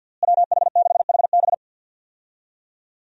Morse code, heard as a keyed tone, about 53 wpm, 700 Hz. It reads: WH6HB
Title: USA Call Signs - 5 Characters - Repeated 3x - 50wpm